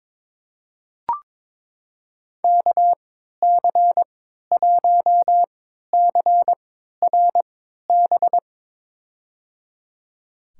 Morse code, heard as a keyed tone, about 22 wpm, spaced as for 12 wpm, 700 Hz. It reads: KC1CRB
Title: USA Call Signs - 6 Characters - 3x Character Spacing - 22wpm